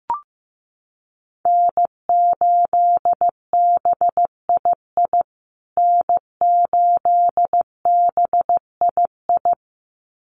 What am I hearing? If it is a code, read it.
Morse code, heard as a keyed tone, about 15 wpm, 700 Hz. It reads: N8BII N8BII
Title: USA Call Signs Repeated 2x 15wpm